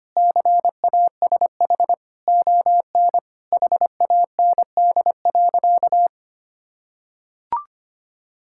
Morse code, heard as a keyed tone, about 25 wpm, 700 Hz. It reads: CASH ON HAND.
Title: Sentences from Top 1000 Words - Review Part 5 - 25wpm